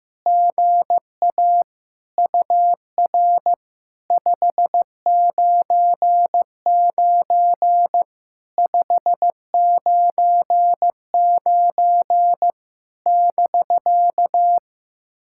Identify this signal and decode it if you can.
Morse code, heard as a keyed tone, about 15 wpm, 700 Hz. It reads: GA UR 599 599 <BK>
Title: SOTA QSO Practice 15wpm